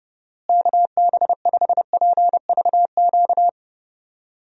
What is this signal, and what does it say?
Morse code, heard as a keyed tone, about 30 wpm, 700 Hz. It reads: KB5PVQ
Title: USA Call Signs - 6 Characters - Repeated 3x - 30wpm